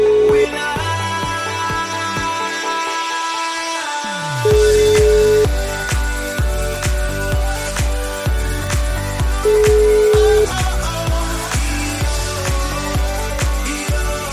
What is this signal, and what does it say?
Ringback tone (Europe, 425 Hz), with noise.